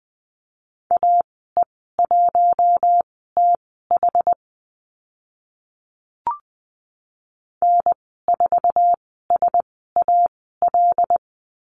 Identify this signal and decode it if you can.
Morse code, heard as a keyed tone, about 20 wpm, spaced as for 14 wpm, 700 Hz. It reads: AE1TH N4SAL
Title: 076 Lesson 13g - Letter L - Call Signs 20wpm